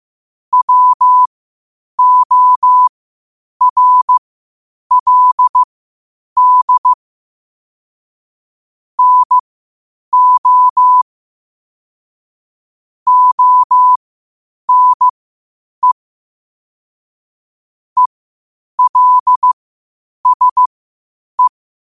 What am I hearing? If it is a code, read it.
Morse code, heard as a keyed tone, about 15 wpm, spaced as for 8 wpm, 1000 Hz. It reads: WORLD NO ONE ELSE
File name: qotdfiles / 8wpm